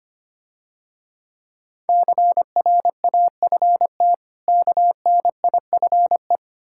Morse code, heard as a keyed tone, about 25 wpm, 700 Hz. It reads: CRAFT KNIFE
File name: practice / 25wpm / Difficult Words - Speed-Racing with Hints - 25wpm